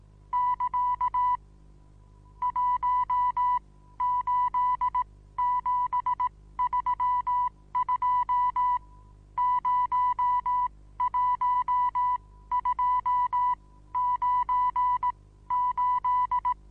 Morse code, heard as a keyed tone, about 18 wpm, spaced as for 14 wpm, 990 Hz. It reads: <KA> 18732 01298
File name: Onlinelehrgang / morsen / DJ4UF Morsen25